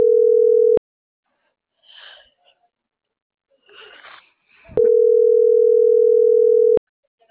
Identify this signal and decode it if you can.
Ringback tone (North America), 440 + 480 Hz.
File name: CENTRAL SAGOT / out xquenda.sagot